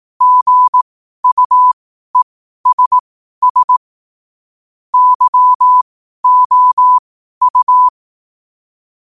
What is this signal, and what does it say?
Morse code, heard as a keyed tone, about 18 wpm, spaced as for 12 wpm, 1000 Hz. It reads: GUESS YOU
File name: qotdfiles / 12wpm